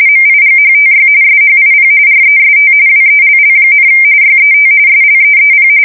RTTY, about 45.45 baud. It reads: MIGHTY SHIP REST WELL, YET SLEEP
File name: fsk170 / USS-New-Jersey